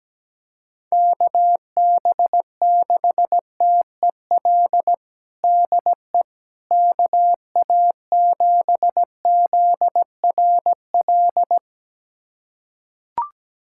Morse code, heard as a keyed tone, about 17 wpm, 700 Hz. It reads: KB6TEL DE KA7ZRL
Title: USA Callsign DE Callsign 17wpm